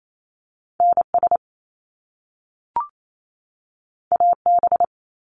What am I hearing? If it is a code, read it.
Morse code, heard as a keyed tone, about 28 wpm, 700 Hz. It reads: NS AB
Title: Canadian Provinces and Territories 28wpm